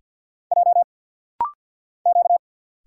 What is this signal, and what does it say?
Morse code, heard as a keyed tone, about 45 wpm, 700 Hz. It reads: J Q
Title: Single Letters - Rapid-Fire 50wpm